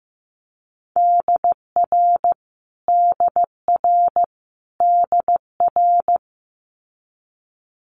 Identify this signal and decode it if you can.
Morse code, heard as a keyed tone, about 15 wpm, 700 Hz. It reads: DR DR DR
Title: Multi-letter Phonemes Repeated 3x 15wpm